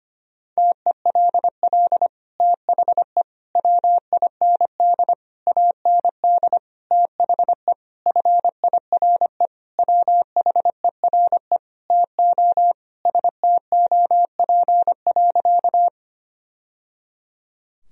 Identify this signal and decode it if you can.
Morse code, heard as a keyed tone, about 25 wpm, 700 Hz. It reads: TELL THE WIND AND THE FIRE WHERE TO STOP.
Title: Sentences from Top 1000 Words - Review Part 2 - 25wpm